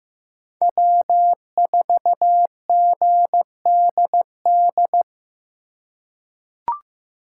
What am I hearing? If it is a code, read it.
Morse code, heard as a keyed tone, about 15 wpm, 700 Hz. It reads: W4GDD
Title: USA Call Signs - 5 Characters - 2x Character Spacing - 15wpm